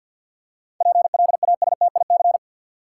Morse code, heard as a keyed tone, about 48 wpm, 700 Hz. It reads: PLASTIC
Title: Seven Letter Words - 50wpm